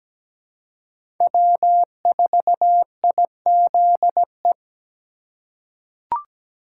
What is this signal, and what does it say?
Morse code, heard as a keyed tone, about 17 wpm, 700 Hz. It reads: W4IZE
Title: USA Call Signs - 5 Characters - 2x Character Spacing - 17wpm